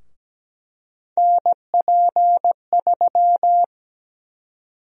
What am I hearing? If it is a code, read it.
Morse code, heard as a keyed tone, about 17 wpm, 700 Hz. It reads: NP3
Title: USA Call Sign Prefixes - 17wpm